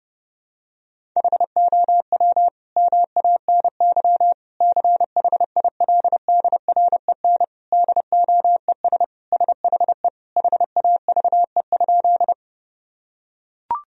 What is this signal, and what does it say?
Morse code, heard as a keyed tone, about 30 wpm, 700 Hz. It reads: HOW MANY CHILDREN DOES SHE HAVE?
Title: Sentences from Top 1000 Words - Review Part 1 - 30wpm